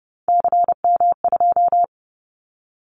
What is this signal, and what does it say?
Morse code, heard as a keyed tone, about 30 wpm, 700 Hz. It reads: CM2
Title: Callsign Prefixes for Top 100 DXCC - 30wpm